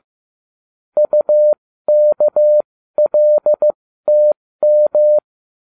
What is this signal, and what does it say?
Morse code, heard as a keyed tone, about 15 wpm, spaced as for 13 wpm, 600 Hz. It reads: UKLTM
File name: cw / ubacw09 1